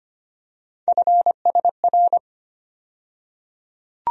Morse code, heard as a keyed tone, about 25 wpm, 700 Hz. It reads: FSR
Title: USA Call Sign Postfixes - 3 Letters 25wpm